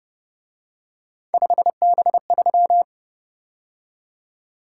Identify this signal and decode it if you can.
Morse code, heard as a keyed tone, about 31 wpm, 700 Hz. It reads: 5B3
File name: Speed-Racing - Three Letter-Number Combinations - 25wpm